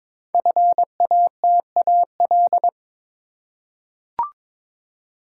Morse code, heard as a keyed tone, about 22 wpm, 700 Hz. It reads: FATAL
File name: 5 Letter Words - First 3 Letters Given - 22wpm